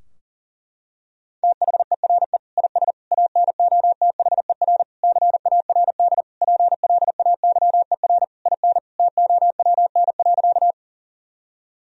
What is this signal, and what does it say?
Morse code, heard as a keyed tone, about 40 wpm, 700 Hz. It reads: THERE IS ANOTHER CARD PLAYER IN TOWN.
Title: Sentences from Top 1000 Words - Review Part 4 - 40wpm